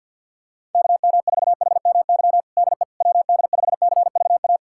Morse code, heard as a keyed tone, about 49 wpm, 700 Hz. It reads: KM4HMY DE WD5CVA
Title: USA Callsign DE Callsign - 6 Characters - 50wpm